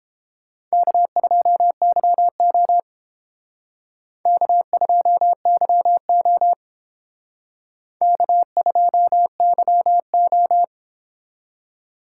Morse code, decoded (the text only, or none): K2YO K2YO K2YO